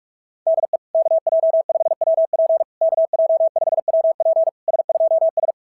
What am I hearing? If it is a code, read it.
Morse code, heard as a keyed tone, about 41 wpm, 660 Hz. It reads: DE KJ5WP KJ5WP S2S
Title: SOTA QSO Practice 45wpm